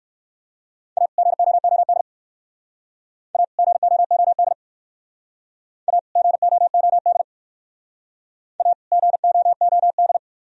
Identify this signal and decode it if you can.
Morse code, heard as a keyed tone, about 58 wpm, 700 Hz. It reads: A GOOD A GOOD A GOOD A GOOD